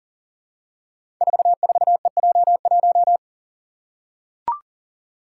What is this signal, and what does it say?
Morse code, heard as a keyed tone, about 41 wpm, 700 Hz. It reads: 44EJ1
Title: Code Groups - 5 Letter:Numbers - 40wpm